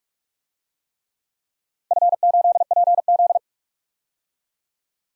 Morse code, heard as a keyed tone, about 46 wpm, 700 Hz. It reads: F8PZ
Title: DXCC-100 Call Signs - 4 Characters - 45wpm